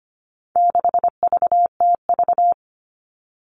Morse code, heard as a keyed tone, about 25 wpm, 700 Hz. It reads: 6VTV
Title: All Cut Numbers - 4-Digit 25wpm